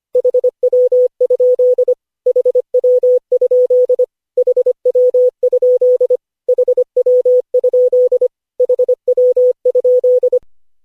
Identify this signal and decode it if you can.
Morse code, heard as a keyed tone, about 25 wpm, 500 Hz. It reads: HW? HW? HW? HW? HW?
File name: cw / Session5 cw training